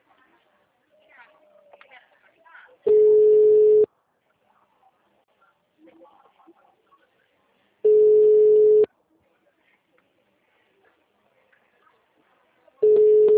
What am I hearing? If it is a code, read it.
Ringback tone (Europe, 425 Hz), with speech.